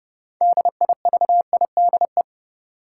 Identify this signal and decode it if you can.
Morse code, heard as a keyed tone, about 30 wpm, 700 Hz. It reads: DIVIDE
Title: Six Letter Words - 30wpm